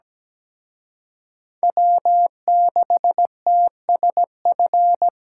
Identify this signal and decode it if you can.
Morse code, heard as a keyed tone, about 20 wpm, 700 Hz. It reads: W6TSF
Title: Speed Racing - USA Call Signs 17wpm